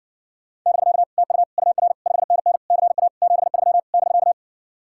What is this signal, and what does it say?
Morse code, heard as a keyed tone, about 70 wpm, 700 Hz. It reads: <BK> TU UR 5NN CA 73 <BK>